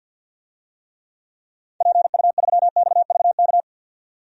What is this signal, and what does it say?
Morse code, heard as a keyed tone, about 52 wpm, 700 Hz. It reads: PU3XVK
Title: DXCC-100 Call Signs - 6 Characters - 50wpm